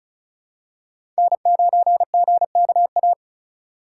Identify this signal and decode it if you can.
Morse code, heard as a keyed tone, about 36 wpm, 700 Hz. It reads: N9GKA
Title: USA Call Signs - 5 Characters - 4x Character Spacing - 35wpm